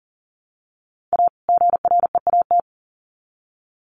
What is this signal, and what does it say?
Morse code, heard as a keyed tone, about 40 wpm, 700 Hz. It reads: A GREAT
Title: Sets of 2 Words 40wpm